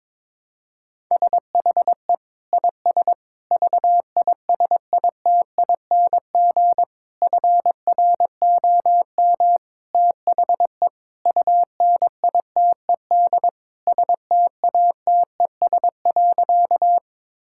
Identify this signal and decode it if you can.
Morse code, heard as a keyed tone, about 22 wpm, 700 Hz. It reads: SHE IS VISITING FROM THE UNITED STATES.
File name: practice / 22wpm / Sentences with 4-Word Phrases - 22wpm